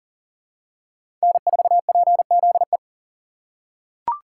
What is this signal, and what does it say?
Morse code, heard as a keyed tone, about 37 wpm, 700 Hz. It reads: N4PZE
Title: USA Call Signs - 2x Character Spacing - 40wpm